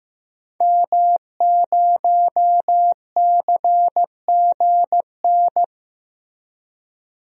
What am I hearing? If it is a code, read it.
Morse code, heard as a keyed tone, about 15 wpm, 700 Hz. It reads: M0CGN
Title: DXCC-100 Call Signs - 5 Characters - Repeated 3x - 15wpm